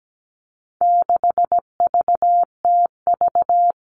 Morse code, heard as a keyed tone, about 17 wpm, 700 Hz. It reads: 6VTV
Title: All Cut Numbers - 4-Digit 17wpm